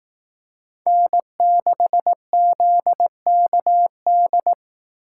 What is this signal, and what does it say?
Morse code, heard as a keyed tone, about 18 wpm, 700 Hz. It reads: N6ZKD
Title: Speed Racing - USA 1x3 Call Signs 15wpm